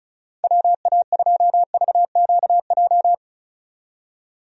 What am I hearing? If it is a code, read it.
Morse code, heard as a keyed tone, about 36 wpm, 700 Hz. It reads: WA2VQJ
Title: USA Call Signs - 6 Characters - 35wpm